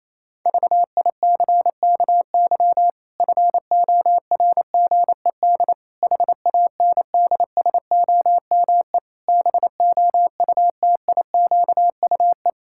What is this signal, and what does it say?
Morse code, heard as a keyed tone, about 27 wpm, 700 Hz. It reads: VICKY FORGED HANDSOME BOUTIQUE
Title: Pangram Challenge v2 - 28wpm